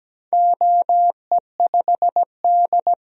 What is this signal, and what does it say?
Morse code, heard as a keyed tone, about 17 wpm, 700 Hz. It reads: OE5D
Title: DXCC-100 Call Signs - 4 Characters - 17wpm